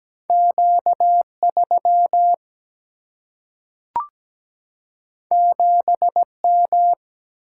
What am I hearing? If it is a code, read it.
Morse code, heard as a keyed tone, about 17 wpm, 700 Hz. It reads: Q3 7M
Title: Single-Letter Single-Number 17wpm